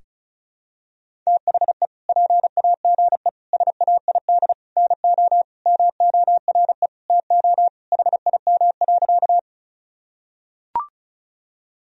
Morse code, heard as a keyed tone, about 36 wpm, 700 Hz. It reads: THE PAGE SAID NO MORE TO HIM.
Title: Sentences from Top 1000 Words - Review Part 1 - 35wpm